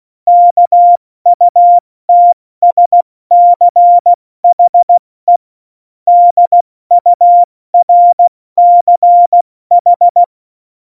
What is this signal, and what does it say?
Morse code, heard as a keyed tone, about 16 wpm, 700 Hz. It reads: KUTSCHE DURCH
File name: Morsekurs / Witze / Witz 19 Mai, 2020 70 F80